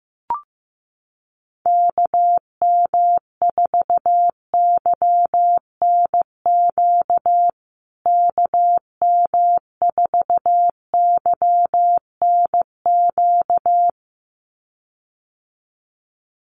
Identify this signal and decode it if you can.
Morse code, heard as a keyed tone, about 15 wpm, 700 Hz. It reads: KM4YNQ KM4YNQ
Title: USA Call Signs Repeated 2x 15wpm